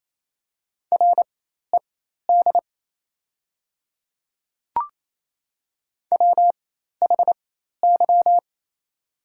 Morse code, heard as a keyed tone, about 28 wpm, spaced as for 13 wpm, 690 Hz. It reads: RED WHY
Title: 4x Character Spacing - Top 500 Words - 28wpm